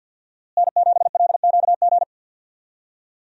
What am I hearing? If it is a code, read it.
Morse code, heard as a keyed tone, about 51 wpm, 700 Hz. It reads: N7LQG
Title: USA Call Signs - 5 Characters - Repeated 3x - 50wpm